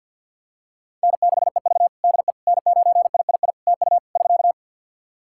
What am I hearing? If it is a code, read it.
Morse code, heard as a keyed tone, about 52 wpm, 700 Hz. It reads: N6EV DE N9III TU <SK>
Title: Practice QSO - Exchange 4 - Ending - 50wpm